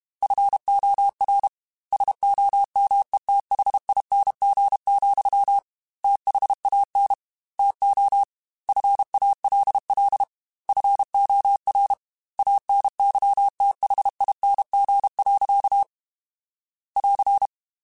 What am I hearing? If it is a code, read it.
Morse code, heard as a keyed tone, about 31 wpm, 800 Hz. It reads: FOR SOMETHING, THAN TO FALL FOR ANYTHING. <AR>
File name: Quotes / Quotes-mp3s / Temper @30